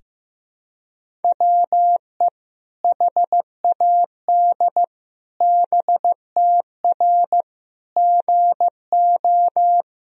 Morse code, heard as a keyed tone, about 15 wpm, 700 Hz. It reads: WE HAD BTR GO
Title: Sentences from Top 400 Words and Abbreviations 15wpm